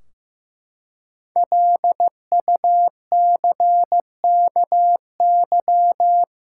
Morse code, heard as a keyed tone, about 16 wpm, 700 Hz. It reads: LUCKY
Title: Five Letter Words - 15wpm